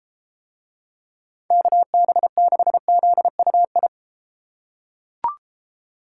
Morse code, heard as a keyed tone, about 33 wpm, 700 Hz. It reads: KB6ZUI